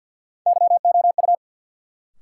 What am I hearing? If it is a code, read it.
Morse code, heard as a keyed tone, about 53 wpm, 700 Hz. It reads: YOU
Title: Sets of 2 Words - New Word Drill - 50wpm